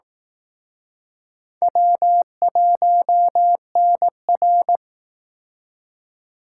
Morse code, heard as a keyed tone, about 19 wpm, 700 Hz. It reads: W1NR